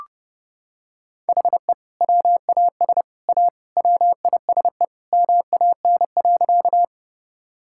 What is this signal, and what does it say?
Morse code, heard as a keyed tone, about 30 wpm, 700 Hz. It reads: HE WAS A WISE MAN.
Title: Sentences from Top 1600 Words 30wpm